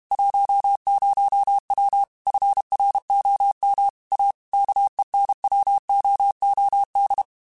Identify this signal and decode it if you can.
Morse code, heard as a keyed tone, about 32 wpm, 800 Hz. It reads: 10W FROM A KENWOOD
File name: QSOs / QSOs-mp3s / kc7sac@30